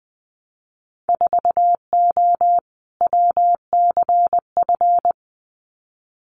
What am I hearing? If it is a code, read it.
Morse code, heard as a keyed tone, about 20 wpm, 700 Hz. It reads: 4O WCF
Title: Winter Field Day Exchanges Encore 20wpm